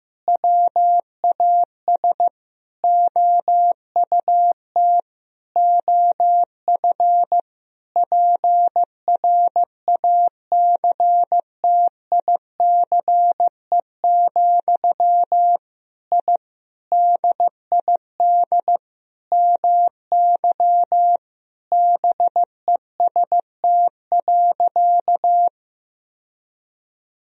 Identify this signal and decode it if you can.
Morse code, heard as a keyed tone, about 16 wpm, 700 Hz. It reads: WAS OUT OF PRACTICE, I DID MY BEST.
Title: Sentences from Top 1000 Words - Review Part 4 - 15wpm